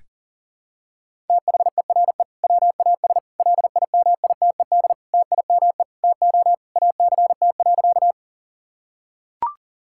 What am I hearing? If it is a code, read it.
Morse code, heard as a keyed tone, about 40 wpm, 700 Hz. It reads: THERE WAS LIMITED TIME TO ACT.
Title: Sentences from Top 1000 Words - Review Part 6 - 40wpm